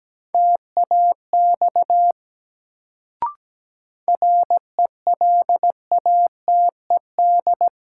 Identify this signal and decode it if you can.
Morse code, heard as a keyed tone, about 17 wpm, 700 Hz. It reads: TAX RELATED
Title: Sentences from Top 1000 Words - Review Part 5 - New Word Drill - 17wpm